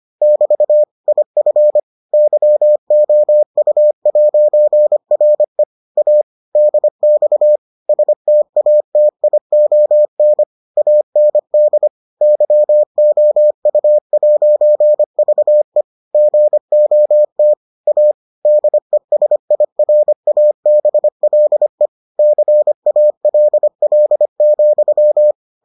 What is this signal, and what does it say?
Morse code, heard as a keyed tone, about 25 wpm, 600 Hz. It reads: <BT> IF YOU'RE A DX STATION AND YOU'VE GOT A DESIRABLE CALL,